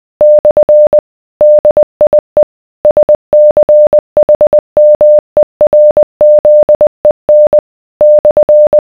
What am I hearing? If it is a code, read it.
Morse code, heard as a keyed tone, about 20 wpm, 600 Hz. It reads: / DIE SCHMERZEN /